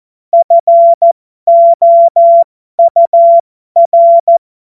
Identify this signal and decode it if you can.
Morse code, heard as a keyed tone, about 14 wpm, spaced as for 12 wpm, 680 Hz. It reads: FOUR
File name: Morsekurs / Witze